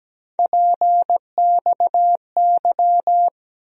Morse code, heard as a keyed tone, about 17 wpm, 700 Hz. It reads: PXY
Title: USA Call Sign Postfixes - 3 Letters 17wpm